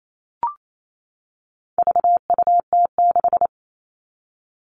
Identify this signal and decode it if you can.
Morse code, heard as a keyed tone, about 28 wpm, 700 Hz. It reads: VUT6
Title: All Cut Numbers - 4-Digit 28wpm